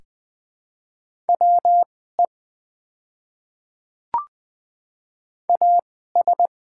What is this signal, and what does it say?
Morse code, heard as a keyed tone, about 20 wpm, spaced as for 14 wpm, 700 Hz. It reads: WE AS
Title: 2x Character Spacing - Top 100 Words - 20wpm